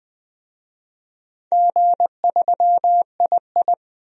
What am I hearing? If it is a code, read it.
Morse code, heard as a keyed tone, about 20 wpm, 700 Hz. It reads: G3II
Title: DXCC-100 Call Signs - 4 Characters - 20wpm